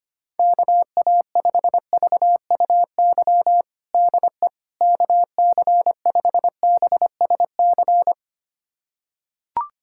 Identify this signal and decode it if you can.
Morse code, heard as a keyed tone, about 25 wpm, 700 Hz. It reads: KA5VUY DE KC5BSC
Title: USA Callsign DE Callsign - 6 Characters - 25wpm